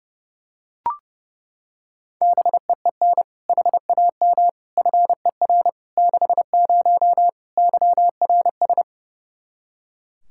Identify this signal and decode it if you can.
Morse code, heard as a keyed tone, about 32 wpm, 700 Hz. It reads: BEEN HAM FER 60 YRS
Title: Practice QSO - Exchange 3 - More Info - 30wpm